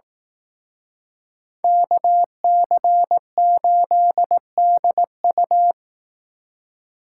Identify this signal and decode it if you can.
Morse code, heard as a keyed tone, about 18 wpm, 700 Hz. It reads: KC8DU
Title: Speed Racing - USA 2x2 Call Signs 15wpm